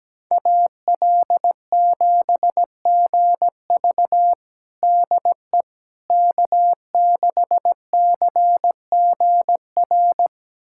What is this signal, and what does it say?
Morse code, heard as a keyed tone, about 17 wpm, 700 Hz. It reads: AL7GV DE K6CGR
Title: USA Callsign DE Callsign - 5 Characters - 17wpm